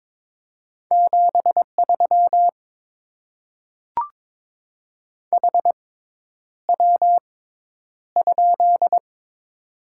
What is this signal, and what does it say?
Morse code, heard as a keyed tone, about 22 wpm, spaced as for 10 wpm, 700 Hz. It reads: 73 HW?
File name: Top 20 QSO Elements - 6x Character Spacing - 22wpm